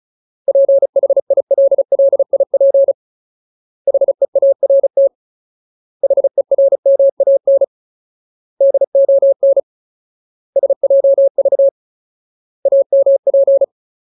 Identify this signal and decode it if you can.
Morse code, heard as a keyed tone, about 34 wpm, 550 Hz. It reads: PHILLIP HEART HERMAN DON SJV AMP
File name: Ultimate Word List / 35WPM Ultimate